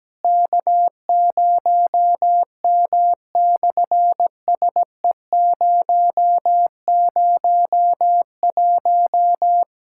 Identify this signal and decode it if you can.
Morse code, heard as a keyed tone, about 17 wpm, 700 Hz. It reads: K0M/SE001
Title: SOTA QSO Practice 17wpm